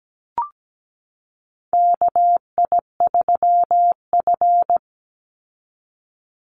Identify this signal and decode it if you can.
Morse code, heard as a keyed tone, about 17 wpm, 700 Hz. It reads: KI3F
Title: Call Signs Repeated 2x - 17wpm